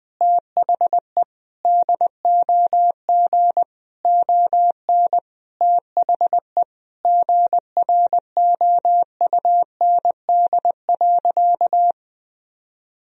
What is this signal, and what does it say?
Morse code, heard as a keyed tone, about 21 wpm, 700 Hz. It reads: THE DOG ON THE GROUND.
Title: Sentences from Top 1000 Words - Review Part 2 - 20wpm